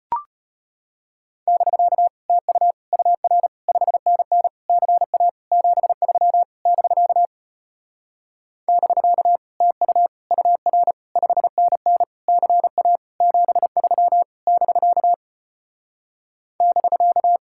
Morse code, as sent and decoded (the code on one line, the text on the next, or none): -...-.- / - ..- / ..- .-. / ..... -. -. / -.-. .- / --... ...-- / -...-.- / -...-.- / - ..- / ..- .-. / ..... -. -. / -.-. .- / --... ...-- / -...-.- / -...-.-
<BK> TU UR 5NN CA 73 <BK> <BK> TU UR 5NN CA 73 <BK> <BK>